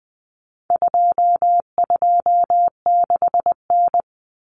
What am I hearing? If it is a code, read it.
Morse code, heard as a keyed tone, about 20 wpm, 700 Hz. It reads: 226N
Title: Common Cut Numbers - 4-Digit 20wpm